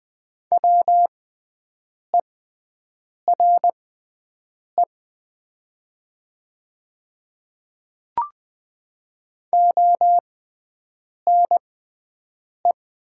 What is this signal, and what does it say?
Morse code, heard as a keyed tone, about 20 wpm, spaced as for 7 wpm, 700 Hz. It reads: WERE ONE
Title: 6x Character Spacing - Top 100 Words - 20wpm